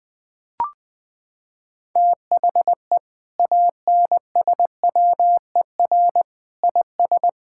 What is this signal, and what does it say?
Morse code, heard as a keyed tone, about 20 wpm, 700 Hz. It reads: THE ANSWER IS
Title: Sentences from Top 1600 Words 20wpm